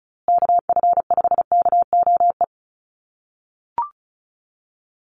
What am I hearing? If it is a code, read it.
Morse code, heard as a keyed tone, about 34 wpm, 700 Hz. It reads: KF5KOE
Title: USA Call Signs Repeated 3x 35wpm